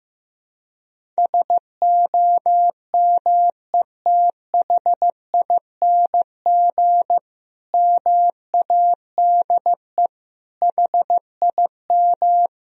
Morse code, heard as a keyed tone, about 15 wpm, 700 Hz. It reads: SOMETHING MADE HIM
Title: Sentences from Top 1000 Words - Review Part 1 - 15wpm